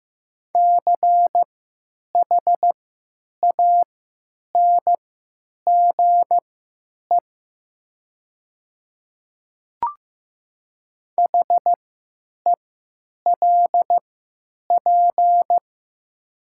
Morse code, heard as a keyed tone, about 15 wpm, spaced as for 9 wpm, 700 Hz. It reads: CHANGE HELP
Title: 3x Character Spacing - Top 200 Words - 15wpm